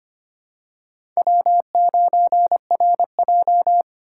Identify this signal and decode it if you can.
Morse code, heard as a keyed tone, about 25 wpm, 700 Hz. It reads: W9RJ
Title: USA Call Signs - 4 Characters - 2x Character Spacing - 25wpm